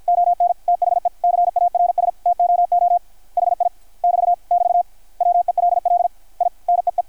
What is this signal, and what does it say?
Morse code, heard as a keyed tone, about 53 wpm, 700 Hz. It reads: ON THE CARS TOO HI <BT> <BT> WELL I NEE
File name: archived / CRSnet